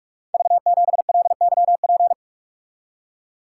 Morse code, heard as a keyed tone, about 44 wpm, 690 Hz. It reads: V7LYP
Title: Code Groups - 5 Letter:Numbers - 45wpm